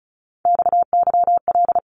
Morse code, heard as a keyed tone, about 34 wpm, 700 Hz. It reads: XYL
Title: Abbreviations 35wpm